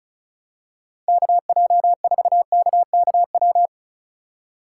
Morse code, heard as a keyed tone, about 34 wpm, 700 Hz. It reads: KJ4KKW
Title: USA Call Signs - 6 Characters - 2x Character Spacing - 35wpm